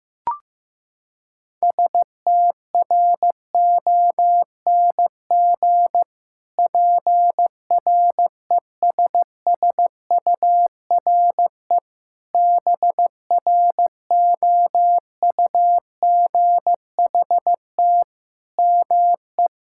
Morse code, heard as a keyed tone, about 15 wpm, 700 Hz. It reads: STRONG PRESSURE BROUGHT ME
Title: Sentences from Top 1000 Words - Review Part 6 - 15wpm